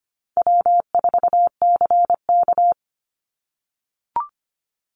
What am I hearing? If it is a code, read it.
Morse code, heard as a keyed tone, about 25 wpm, 700 Hz. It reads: W4CK
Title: Call Signs - 25wpm